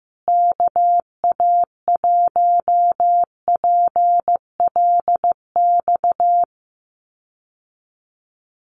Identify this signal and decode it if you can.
Morse code, heard as a keyed tone, about 15 wpm, 700 Hz. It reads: KA1PLX
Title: USA Call Signs Repeated 3x 15wpm